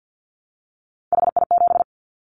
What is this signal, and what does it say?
Morse code, heard as a keyed tone, about 50 wpm, 700 Hz. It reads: HI7
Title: Callsign Prefixes for Top 100 DXCC - 50wpm